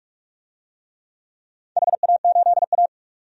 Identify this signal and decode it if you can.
Morse code, heard as a keyed tone, about 46 wpm, 700 Hz. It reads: HA8A
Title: DXCC-100 Call Signs - 4 Characters - 45wpm